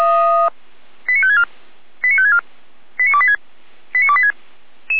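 Five-tone selective call (ZVEI), heard as quick runs of five tones, tones of about 70 ms. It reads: 89564 89564 89286 89286
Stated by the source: ZVEI five-tone selective call received on SDR.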